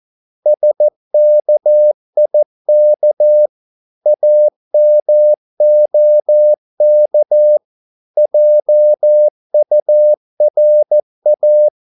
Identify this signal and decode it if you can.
Morse code, heard as a keyed tone, about 14 wpm, 600 Hz. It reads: SKIK AMOK JURA
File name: Korte ord DK 14wpm